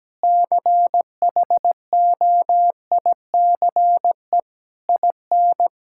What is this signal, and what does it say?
Morse code, heard as a keyed tone, about 17 wpm, 700 Hz. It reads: CHOICE IN
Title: Sentences from Top 1000 Words - Review Part 6 - 17wpm